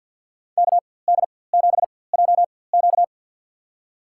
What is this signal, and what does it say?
Morse code, heard as a keyed tone, about 52 wpm, spaced as for 23 wpm, 700 Hz. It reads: KD7JQ
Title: USA Call Signs - 5 Characters - 4x Character Spacing - 50wpm